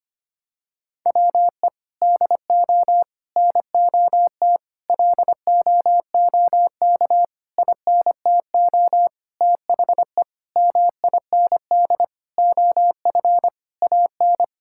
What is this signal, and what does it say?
Morse code, heard as a keyed tone, about 25 wpm, 700 Hz. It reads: WE DO NOT LOOK INTO THE MIND OF AN
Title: Sentences from Top 1600 Words 25wpm